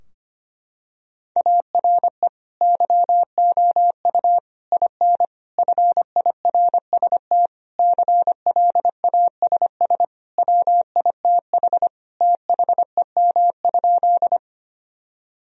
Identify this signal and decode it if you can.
Morse code, heard as a keyed tone, about 25 wpm, 700 Hz. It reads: ARE YOU IN FIRST CLASS WITH THEM?
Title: Sentences from Top 1000 Words - Review Part 2 - 25wpm